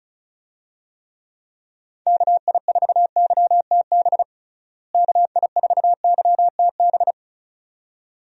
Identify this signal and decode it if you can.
Morse code, heard as a keyed tone, about 36 wpm, 700 Hz. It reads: KI4YTB KI4YTB
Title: USA Call Signs - 6 Characters - Repeated 2x - 35wpm